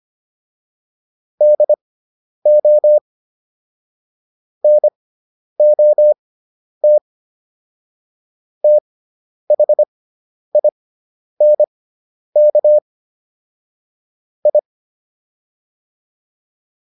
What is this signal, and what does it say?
Morse code, heard as a keyed tone, about 25 wpm, spaced as for 10 wpm, 600 Hz. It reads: DO NOT THINK I